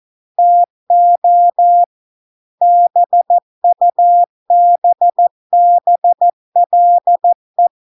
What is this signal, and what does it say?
Morse code, heard as a keyed tone, about 14 wpm, 700 Hz. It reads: TO BUBBLE